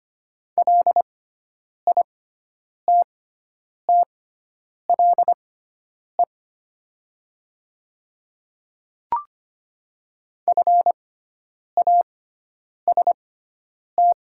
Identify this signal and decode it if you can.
Morse code, heard as a keyed tone, about 25 wpm, spaced as for 9 wpm, 690 Hz. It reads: LITTLE FAST
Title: 6x Character Spacing - Top 500 Words - 25wpm